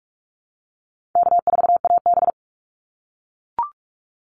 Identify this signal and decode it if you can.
Morse code, heard as a keyed tone, about 44 wpm, 700 Hz. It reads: K4AB
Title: Easy Call Signs - 45wpm